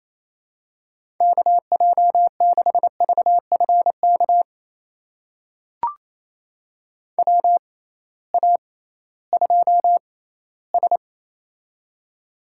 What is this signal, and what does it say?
Morse code, heard as a keyed tone, about 28 wpm, spaced as for 16 wpm, 700 Hz. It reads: KJ6VFK WA2S
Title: USA Call Signs - 6x Character Spacing - 28wpm